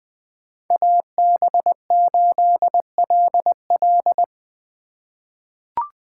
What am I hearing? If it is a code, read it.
Morse code, heard as a keyed tone, about 20 wpm, 700 Hz. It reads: AB8LL
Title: 2023 POTA Hunter Call Signs - 20wpm